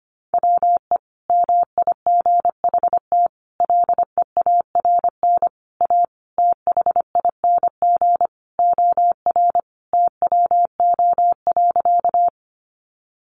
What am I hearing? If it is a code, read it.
Morse code, heard as a keyed tone, about 25 wpm, 700 Hz. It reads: WE MIGHT LEARN A THING OR TWO.
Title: Sentences from Top 300 Words 25wpm